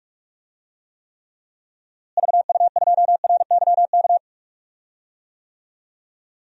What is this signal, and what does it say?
Morse code, heard as a keyed tone, about 41 wpm, 700 Hz. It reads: VU2RYK
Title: DXCC-100 Call Signs - 6 Characters - Repeated 3x - 45wpm